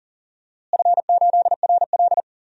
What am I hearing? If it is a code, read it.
Morse code, heard as a keyed tone, about 40 wpm, 690 Hz. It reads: F8RL
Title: DXCC-100 Call Signs - 4 Characters - Repeated 2x - 40wpm